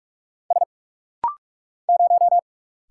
Morse code, heard as a keyed tone, about 46 wpm, 700 Hz. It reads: S 0
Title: Single Letter-Number Alt - Rapid-Fire 45wpm